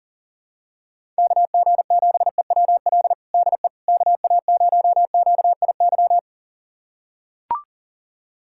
Morse code, heard as a keyed tone, about 40 wpm, 700 Hz. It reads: KG7EWL DE KA0QIY
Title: USA Callsign DE Callsign - 6 Characters - 40wpm